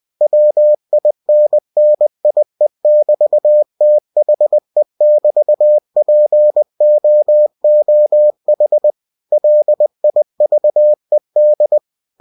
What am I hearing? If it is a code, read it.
Morse code, heard as a keyed tone, about 20 wpm, 600 Hz. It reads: WINNIE<BT>THE<BT>POOH LIVED